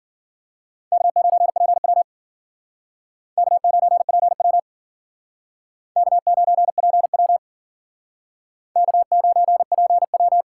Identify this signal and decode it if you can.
Morse code, heard as a keyed tone, about 53 wpm, 700 Hz. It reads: K9PW K9PW K9PW K9PW